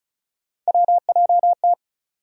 Morse code, heard as a keyed tone, about 36 wpm, 700 Hz. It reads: WJT
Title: USA Call Sign Postfixes - 3 Letters 35wpm